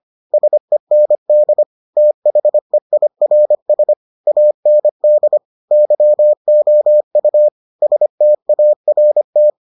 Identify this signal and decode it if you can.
Morse code, heard as a keyed tone, about 25 wpm, 600 Hz. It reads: SEND THEIRS AND YOU START